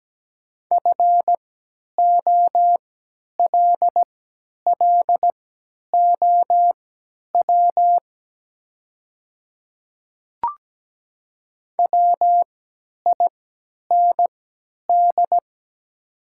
Morse code, heard as a keyed tone, about 17 wpm, spaced as for 10 wpm, 700 Hz. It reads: FOLLOW WIND
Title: 3x Character Spacing - Top 500 Words - 17wpm